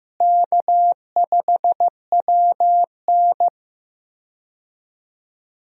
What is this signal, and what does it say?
Morse code, heard as a keyed tone, about 16 wpm, 700 Hz. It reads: K5WN
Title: USA Call Signs - 4 Characters - Repeated 3x - 15wpm